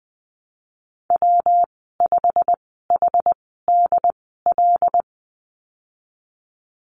Morse code, heard as a keyed tone, about 19 wpm, spaced as for 14 wpm, 700 Hz. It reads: W5HDL
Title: 154 Lesson 25g - Letter p - Call Signs 20wpm